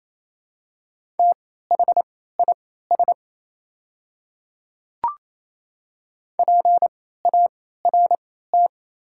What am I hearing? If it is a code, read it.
Morse code, heard as a keyed tone, about 28 wpm, spaced as for 16 wpm, 690 Hz. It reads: THIS PART
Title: 3x Character Spacing - Top 200 Words - 28wpm